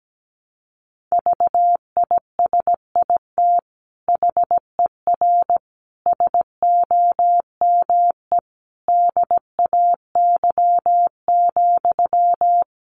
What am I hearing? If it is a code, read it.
Morse code, heard as a keyed tone, about 17 wpm, 700 Hz. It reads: VISIT HER SOME DAY,
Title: Sentences from Top 900 Words 17wpm